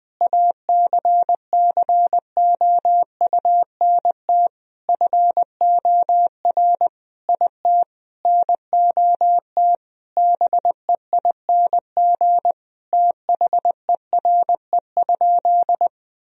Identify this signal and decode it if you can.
Morse code, heard as a keyed tone, about 20 wpm, 700 Hz. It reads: ACCOUNT FOR IT NOT BEING THERE?
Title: Sentences from Top 1000 Words - Review Part 3 - 20wpm